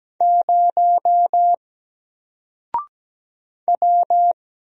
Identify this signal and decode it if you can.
Morse code, heard as a keyed tone, about 17 wpm, 700 Hz. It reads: 0 W
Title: Single Character (Letter:Number:Punctuation:Prosign) 17wpm